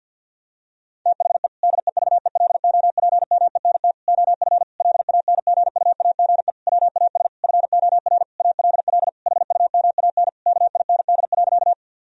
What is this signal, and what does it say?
Morse code, heard as a keyed tone, about 51 wpm, 700 Hz. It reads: THE DEVELOPMENT OF LANGUAGE WAS FOR ALL HUMAN KIND.
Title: Sentences from Top 1000 Words - Review Part 4 - 50wpm